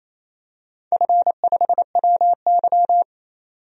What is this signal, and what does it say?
Morse code, heard as a keyed tone, about 28 wpm, 700 Hz. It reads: F5WY
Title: DXCC-100 Call Signs - 4 Characters - 28wpm